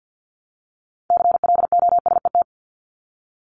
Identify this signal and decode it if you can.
Morse code, heard as a keyed tone, about 50 wpm, 700 Hz. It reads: CLOSET
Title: Top 1751-2000 Words 50wpm